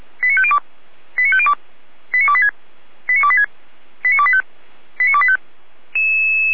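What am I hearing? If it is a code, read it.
Five-tone selective call (ZVEI), heard as quick runs of five tones, tones of about 70 ms. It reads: 89502 89502 89286 89286 89285 89285
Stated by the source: ZVEI five-tone call received by SDR, archived by the Orlamünde fire brigade (feuerwehr-orlamuende.org) server.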